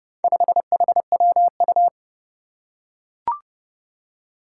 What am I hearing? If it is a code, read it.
Morse code, heard as a keyed tone, about 30 wpm, 700 Hz. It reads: 5HWU
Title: Four Challenging Letter-Number Combinations 30wpm